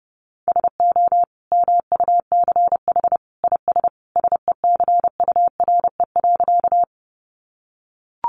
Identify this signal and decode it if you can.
Morse code, heard as a keyed tone, about 30 wpm, 700 Hz. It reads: SO MUCH IS SECURE.
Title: Sentences from Top 1200 Words 30wpm